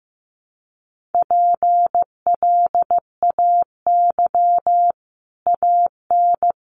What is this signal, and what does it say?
Morse code, heard as a keyed tone, about 15 wpm, 700 Hz. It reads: PLAY AN
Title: Sets of 5 Words Encore 15wpm